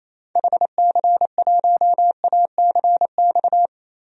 Morse code, heard as a keyed tone, about 27 wpm, 700 Hz. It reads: HC1ACX
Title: DXCC-100 Call Signs - 6 Characters - 28wpm